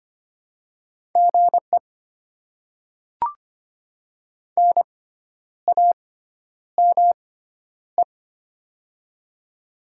Morse code, heard as a keyed tone, about 25 wpm, spaced as for 10 wpm, 700 Hz. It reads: GE NAME
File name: Top 20 QSO Elements - 6x Character Spacing - 25wpm